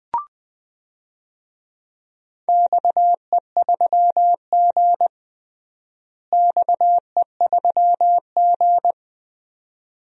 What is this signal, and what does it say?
Morse code, heard as a keyed tone, about 20 wpm, 700 Hz. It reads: XE3G XE3G
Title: DXCC-100 Call Signs - 4 Characters - Repeated 2x - 20wpm